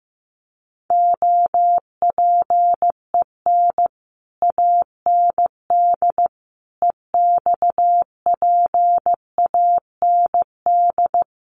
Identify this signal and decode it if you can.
Morse code, heard as a keyed tone, about 15 wpm, 700 Hz. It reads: OPEN AND EXPAND
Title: Sentences from Top 1400 Words 15wpm